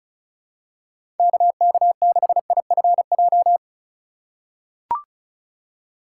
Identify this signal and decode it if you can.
Morse code, heard as a keyed tone, about 36 wpm, 700 Hz. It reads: KK6IFJ
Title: USA Call Signs - 2x Character Spacing - 35wpm